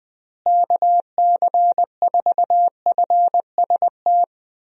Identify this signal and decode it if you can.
Morse code, heard as a keyed tone, about 20 wpm, 700 Hz. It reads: KC4FST
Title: USA Call Signs - 6 Characters - Repeated 2x - 20wpm